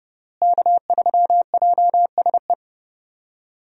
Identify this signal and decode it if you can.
Morse code, heard as a keyed tone, about 30 wpm, 700 Hz. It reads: K3JSE
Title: USA Call Signs - 5 Characters - Repeated 3x - 30wpm